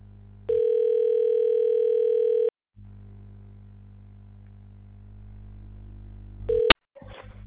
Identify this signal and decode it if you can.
Ringback tone (North America), 440 + 480 Hz.
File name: recordings / RE955b15ccc8486d34e40f287b9141b59d